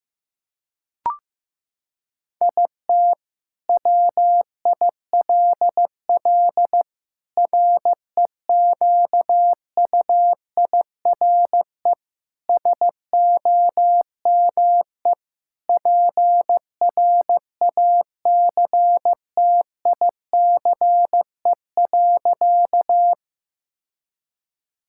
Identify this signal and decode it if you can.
Morse code, heard as a keyed tone, about 14 wpm, 700 Hz. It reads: IT WILL REQUIRE SOME PRACTICE.
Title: Sentences from Top 1000 Words - Review Part 6 - 15wpm